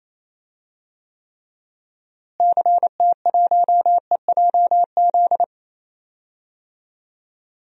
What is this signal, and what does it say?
Morse code, heard as a keyed tone, about 28 wpm, 700 Hz. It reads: CT1EJZ
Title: DXCC-100 Call Signs - 6 Characters - Repeated 2x - 28wpm